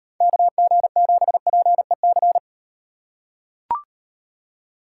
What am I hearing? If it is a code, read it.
Morse code, heard as a keyed tone, about 38 wpm, 700 Hz. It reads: KG7PEC